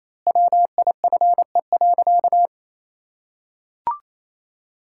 Morse code, heard as a keyed tone, about 28 wpm, 700 Hz. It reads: WIFE.
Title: Pangram Challenge v2 - 28wpm